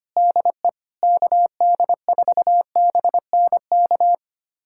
Morse code, heard as a keyed tone, about 25 wpm, 700 Hz. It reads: DE KD4BNK
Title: USA Callsign DE Callsign - 6 Characters - 25wpm